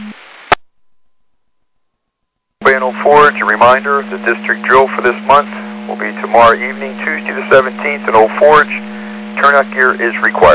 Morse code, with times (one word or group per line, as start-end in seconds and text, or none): none